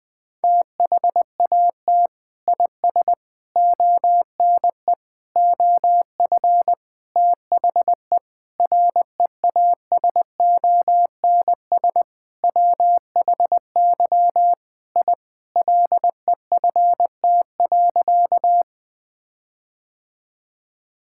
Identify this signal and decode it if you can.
Morse code, heard as a keyed tone, about 19 wpm, 700 Hz. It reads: THAT IS ONE OF THE REASONS WHY I LEFT.
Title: Sentences with 5-Word Phrases - Spoken Beginning - 20wpm